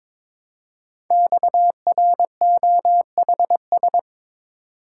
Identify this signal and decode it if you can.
Morse code, heard as a keyed tone, about 22 wpm, 700 Hz. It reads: XROHS
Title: Code Groups - 5 Letters - 22wpm